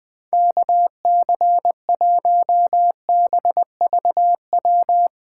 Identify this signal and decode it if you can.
Morse code, heard as a keyed tone, about 21 wpm, 700 Hz. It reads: KC1BVW
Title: USA Call Signs - 6 Characters - 3x Character Spacing - 20wpm